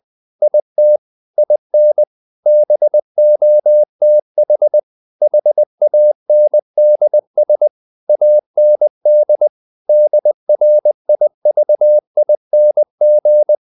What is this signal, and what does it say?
Morse code, heard as a keyed tone, about 20 wpm, 600 Hz. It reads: IT IN BOTH HANDS AND DRIVING
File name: the old man and the sea part 2 20wpm